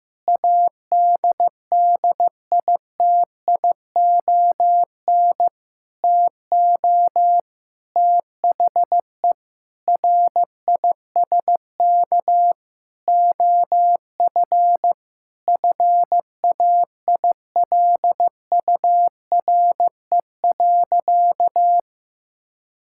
Morse code, heard as a keyed tone, about 15 wpm, 700 Hz. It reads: ADDITION TO THE RISK OF FAILURE.
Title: Sentences from Top 1000 Words - Review Part 6 - 15wpm